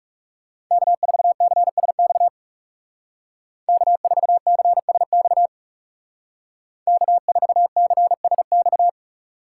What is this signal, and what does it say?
Morse code, heard as a keyed tone, about 36 wpm, 700 Hz. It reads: K4CSX K4CSX K4CSX
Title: Speed Racing - USA 1x3 Call Signs 30wpm